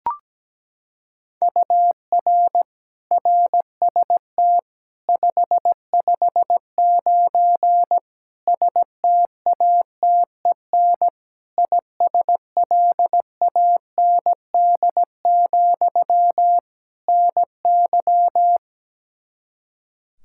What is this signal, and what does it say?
Morse code, heard as a keyed tone, about 17 wpm, 700 Hz. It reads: UR RST 559 STATEN ISLAND, NY
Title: Practice QSO - Exchange 1 - Beginning - 17wpm